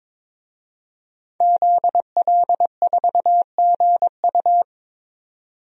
Morse code, heard as a keyed tone, about 22 wpm, 700 Hz. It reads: ZL4GU
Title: DXCC-100 Call Signs - 5 Characters - 22wpm